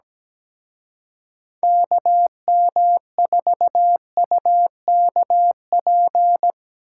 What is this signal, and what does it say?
Morse code, heard as a keyed tone, about 19 wpm, 700 Hz. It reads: KM4UKP
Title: Speed Racing - USA 2x3 Call Signs 17wpm